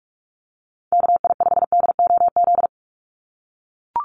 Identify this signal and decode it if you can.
Morse code, heard as a keyed tone, about 44 wpm, 700 Hz. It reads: KI5DOZ
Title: USA Call Signs 45wpm